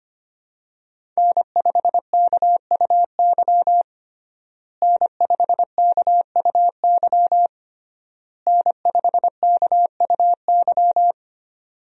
Morse code, heard as a keyed tone, about 25 wpm, 700 Hz. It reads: N5KUY N5KUY N5KUY
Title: USA Call Signs - 5 Characters - Repeated 3x - 25wpm